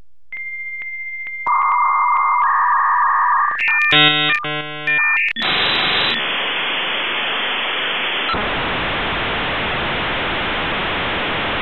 Dial-up modem handshake.
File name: oldschool-modem-sound